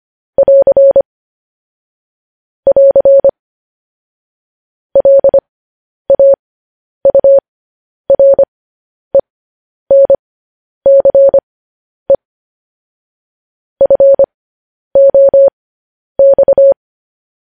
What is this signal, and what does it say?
Morse code, heard as a keyed tone, about 25 wpm, spaced as for 10 wpm, 550 Hz. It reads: <AR> <AR> LAURENCE FOX
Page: news-headlines-in-morse-code-at-10-wpm